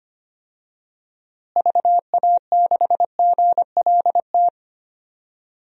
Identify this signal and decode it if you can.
Morse code, heard as a keyed tone, about 25 wpm, 700 Hz. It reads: VA6GLT
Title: DXCC-100 Call Signs - 6 Characters - Repeated 2x - 25wpm